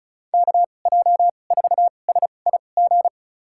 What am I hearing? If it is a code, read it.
Morse code, heard as a keyed tone, about 36 wpm, spaced as for 25 wpm, 700 Hz. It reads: KJ4SIG
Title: USA Call Signs - 6 Characters - 2x Character Spacing - 35wpm